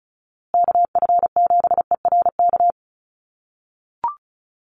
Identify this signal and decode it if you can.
Morse code, heard as a keyed tone, about 34 wpm, 700 Hz. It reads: KF7ERK
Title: USA Call Signs Repeated 3x 35wpm